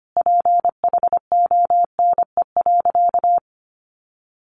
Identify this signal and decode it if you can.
Morse code, heard as a keyed tone, about 25 wpm, 700 Hz. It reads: PHONE.
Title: Sentences from Top 1100 Words 25wpm